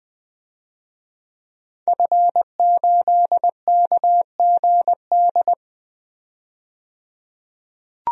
Morse code, heard as a keyed tone, about 20 wpm, 700 Hz. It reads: F8KGD
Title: DXCC-100 Call Signs - 5 Characters - 20wpm